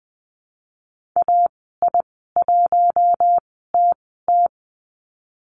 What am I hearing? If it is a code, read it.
Morse code, heard as a keyed tone, about 20 wpm, spaced as for 14 wpm, 700 Hz. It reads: AI1TT
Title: 055 Lesson 10g - Letter r - Call Signs 20wpm